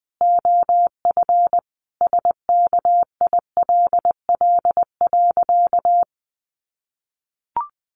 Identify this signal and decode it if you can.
Morse code, heard as a keyed tone, about 20 wpm, 700 Hz. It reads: OF SKILL.
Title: Sentences from Top 1200 Words 20wpm